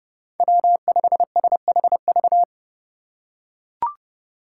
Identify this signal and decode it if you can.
Morse code, heard as a keyed tone, about 30 wpm, 700 Hz. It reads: W5SHV
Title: USA Call Signs - 5 Characters - 4x Character Spacing - 30wpm